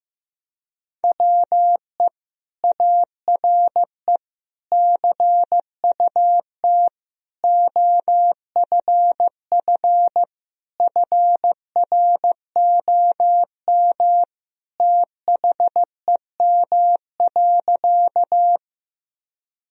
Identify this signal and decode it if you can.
Morse code, heard as a keyed tone, about 15 wpm, 700 Hz. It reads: WE ARE CUT OFF FROM THEM.
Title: Sentences from Top 1000 Words - Review Part 1 - 15wpm